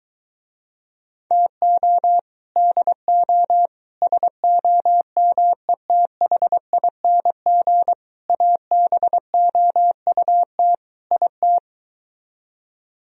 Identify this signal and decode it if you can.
Morse code, heard as a keyed tone, about 24 wpm, 700 Hz. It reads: TO DO SOMETHING ABOUT IT